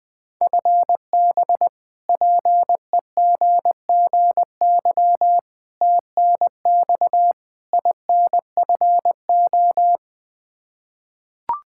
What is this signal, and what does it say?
Morse code, heard as a keyed tone, about 20 wpm, 700 Hz. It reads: FB PEGGY TNX INFO
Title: Practice QSO - Exchange 3 - More Info - 20wpm